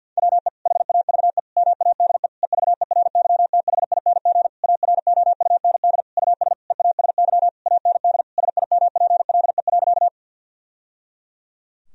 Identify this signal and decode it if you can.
Morse code, heard as a keyed tone, about 50 wpm, 700 Hz. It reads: WE HAVE MADE EVERYTHING AROUND US EASY AND SIMPLE.
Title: Sentences from Top 1000 Words - Review Part 3 - 50wpm